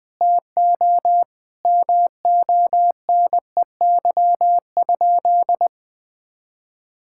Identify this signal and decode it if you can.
Morse code, heard as a keyed tone, about 20 wpm, 700 Hz. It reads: TO MONEY?
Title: Sentences from Top 1000 Words - Review Part 6 - 20wpm